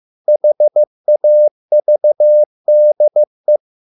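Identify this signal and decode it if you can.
Morse code, heard as a keyed tone, about 15 wpm, 600 Hz. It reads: HAVDE
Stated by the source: Morse code at a slow 15 wpm from CW med Gnister.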